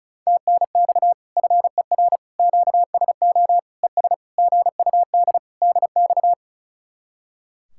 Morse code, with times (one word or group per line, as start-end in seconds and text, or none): 0.3-1.1 TNX
1.4-2.2 FER
2.4-3.6 QSO
3.8-4.1 ES
4.4-5.4 GUD
5.6-6.3 DX